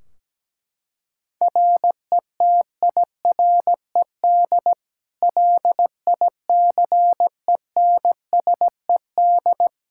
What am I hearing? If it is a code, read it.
Morse code, heard as a keyed tone, about 17 wpm, 700 Hz. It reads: RETIRED LICENSED
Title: Practice QSO - Exchange 3 - More Info - 17wpm